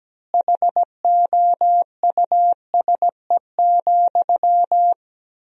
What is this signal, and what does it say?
Morse code, heard as a keyed tone, about 17 wpm, 700 Hz. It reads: HOUSE,
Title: 2x Word-Spacing - Sentences from Top 400 Words - 17wpm